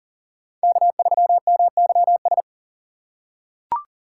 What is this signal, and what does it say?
Morse code, heard as a keyed tone, about 41 wpm, 700 Hz. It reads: K3MYS
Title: USA Call Signs - 5 Characters - 6x Character Spacing - 40wpm